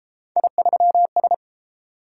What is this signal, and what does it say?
Morse code, heard as a keyed tone, about 28 wpm, 700 Hz. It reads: I3S
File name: Speed-Racing - Three Letter-Number Combinations - 22wpm